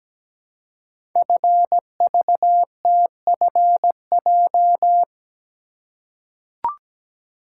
Morse code, heard as a keyed tone, about 17 wpm, 700 Hz. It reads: FVTFJ
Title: Code Groups - 5 Letters - 17wpm